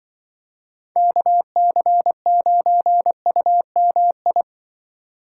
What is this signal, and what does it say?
Morse code, heard as a keyed tone, about 25 wpm, 700 Hz. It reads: KC9UMI
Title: Speed Racing - USA 2x3 Call Signs 20wpm